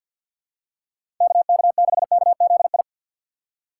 Morse code, heard as a keyed tone, about 48 wpm, 700 Hz. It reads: KK6KZI
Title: USA Call Signs - 6x Character Spacing - 50wpm